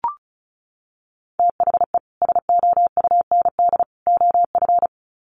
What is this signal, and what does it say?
Morse code, heard as a keyed tone, about 34 wpm, 700 Hz. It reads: THE SOUND OF
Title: Sets of 3 Words 35wpm